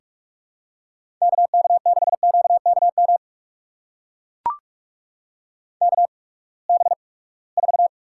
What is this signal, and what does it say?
Morse code, heard as a keyed tone, about 41 wpm, spaced as for 25 wpm, 700 Hz. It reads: KK6QKM KB4
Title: USA Call Signs - 6 Characters - 8x Character Spacing - 45wpm